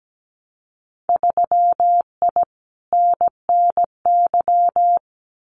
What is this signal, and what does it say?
Morse code, heard as a keyed tone, about 17 wpm, 700 Hz. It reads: 3I NNY
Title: Winter Field Day Exchanges 17wpm